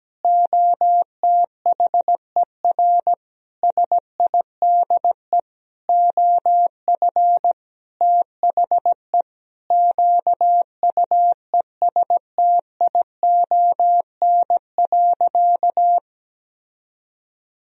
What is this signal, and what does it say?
Morse code, heard as a keyed tone, about 17 wpm, 700 Hz. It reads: OTHER SIDE OF THE QUESTION.
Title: Sentences from Top 1000 Words - Review Part 1 - 17wpm